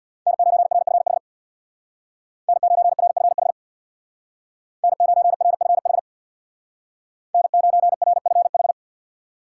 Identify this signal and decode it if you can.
Morse code, heard as a keyed tone, about 65 wpm, 700 Hz. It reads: N9RFH N9RFH N9RFH N9RFH